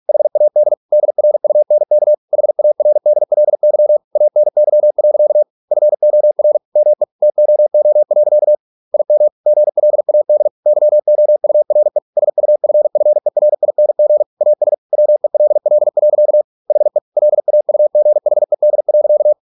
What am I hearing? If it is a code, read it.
Morse code, heard as a keyed tone, about 46 wpm, 600 Hz. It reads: HAD DRUNK HARDLY ANY. FOR ME TOO. IM GLAD YOURE SUFFERING AS WELL. HE LAUGHED.